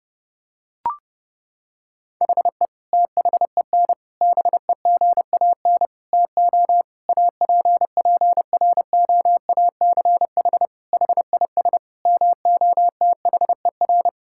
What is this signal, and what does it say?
Morse code, heard as a keyed tone, about 30 wpm, 700 Hz. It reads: HE THEN BEGAN TO APPROACH HIS MOTHER
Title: Sentences from Top 1000 Words - Review Part 6 - 30wpm